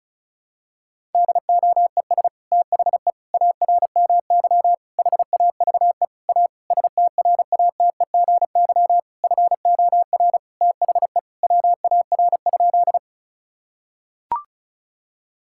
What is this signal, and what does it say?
Morse code, heard as a keyed tone, about 36 wpm, 700 Hz. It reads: DOES THE ARMY HAVE A STRATEGY FOR THE WAR?
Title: Sentences from Top 1000 Words - Review Part 5 - 35wpm